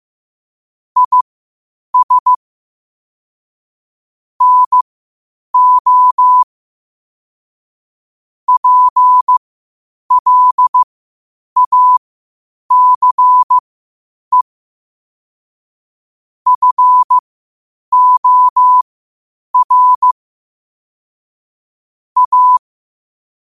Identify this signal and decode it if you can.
Morse code, heard as a keyed tone, about 15 wpm, spaced as for 8 wpm, 1000 Hz. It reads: IS NO PLACE FOR A
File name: qotdfiles / 8wpm